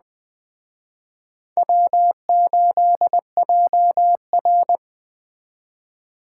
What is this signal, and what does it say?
Morse code, heard as a keyed tone, about 20 wpm, 700 Hz. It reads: W8JR
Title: Speed Racing - USA Call Signs 17wpm